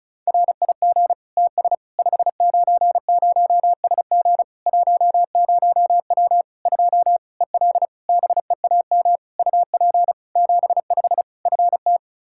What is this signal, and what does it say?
Morse code, heard as a keyed tone, about 36 wpm, 700 Hz. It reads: RIG TS 590SG 10W 2 EL BEAM UP 75 FT
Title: Practice QSO - Exchange 2 - Info - 35wpm